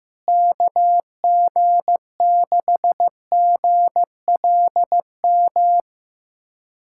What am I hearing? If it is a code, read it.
Morse code, heard as a keyed tone, about 15 wpm, 700 Hz. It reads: KG6GLM
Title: USA Callsign DE Callsign - 6 Characters - 15wpm